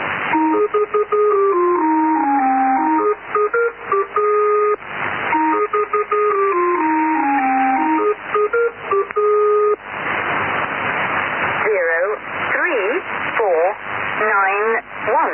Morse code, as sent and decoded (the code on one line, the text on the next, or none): none